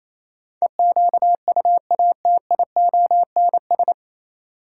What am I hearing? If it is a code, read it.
Morse code, heard as a keyed tone, about 27 wpm, 700 Hz. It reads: EQUATIONS
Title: Sets of 5 Words - New Word Drill - 28wpm